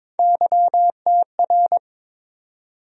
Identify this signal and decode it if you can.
Morse code, heard as a keyed tone, about 22 wpm, 700 Hz. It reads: YTR
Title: USA Call Sign Postfixes - 3 Letters 22wpm